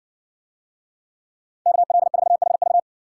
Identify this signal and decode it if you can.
Morse code, heard as a keyed tone, about 55 wpm, 700 Hz. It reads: KL4HV